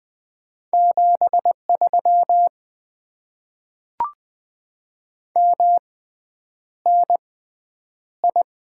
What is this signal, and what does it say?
Morse code, heard as a keyed tone, about 20 wpm, spaced as for 9 wpm, 700 Hz. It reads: 73 MNI
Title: Top 90 QSO Elements - 6x Character Spacing - 20wpm